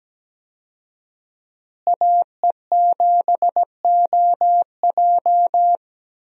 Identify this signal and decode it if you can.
Morse code, heard as a keyed tone, about 17 wpm, 700 Hz. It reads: AE7OJ
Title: USA Call Signs - 5 Characters - Repeated 3x - 17wpm